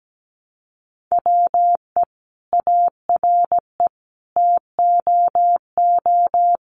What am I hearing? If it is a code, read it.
Morse code, heard as a keyed tone, about 17 wpm, 700 Hz. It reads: WE ARE TOO
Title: Sentences from Top 1100 Words 17wpm